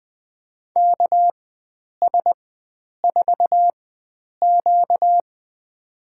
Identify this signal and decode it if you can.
Morse code, heard as a keyed tone, about 20 wpm, spaced as for 9 wpm, 700 Hz. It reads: KS4Q
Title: USA Call Signs - 4 Characters - 4x Character Spacing - 20wpm